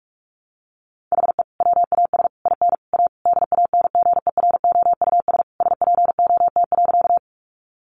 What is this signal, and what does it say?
Morse code, heard as a keyed tone, about 44 wpm, 700 Hz. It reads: HE WAS IN A DANGEROUS SPOT.
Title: Sentences from Top 1200 Words 45wpm